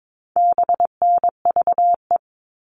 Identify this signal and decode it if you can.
Morse code, heard as a keyed tone, about 22 wpm, 700 Hz. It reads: BNVE
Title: All Cut Numbers - 4-Digit 22wpm